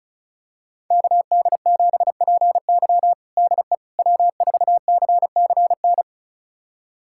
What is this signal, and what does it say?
Morse code, heard as a keyed tone, about 34 wpm, 700 Hz. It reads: KD7PY DE W4CCN
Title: USA Callsign DE Callsign - 5 Characters - 35wpm